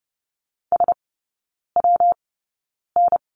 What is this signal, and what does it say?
Morse code, heard as a keyed tone, about 30 wpm, spaced as for 9 wpm, 700 Hz. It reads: SWN
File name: Lesson 19e - Letter w - Three Characters 30wpm